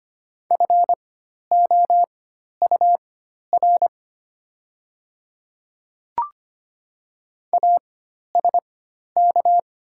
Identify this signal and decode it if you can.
Morse code, heard as a keyed tone, about 25 wpm, spaced as for 12 wpm, 700 Hz. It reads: FOUR ASK
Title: 4x Character Spacing - Top 300 Words - 25wpm